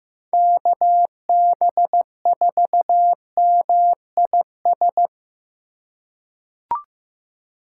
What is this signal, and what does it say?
Morse code, heard as a keyed tone, about 16 wpm, 700 Hz. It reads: KB4MIS
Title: USA Call Signs - 6 Characters - 4x Character Spacing - 15wpm